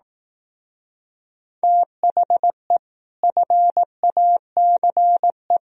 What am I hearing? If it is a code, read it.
Morse code, heard as a keyed tone, about 18 wpm, 700 Hz. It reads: THE FACE
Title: Speed-Racing - Sets of 3 Words 15wpm